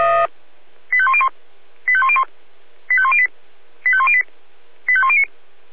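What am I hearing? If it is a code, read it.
Five-tone selective call (ZVEI), heard as quick runs of five tones, tones of about 70 ms. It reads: 85101 85101 85108 85108 85109
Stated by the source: Five-tone ZVEI selective call received by SDR.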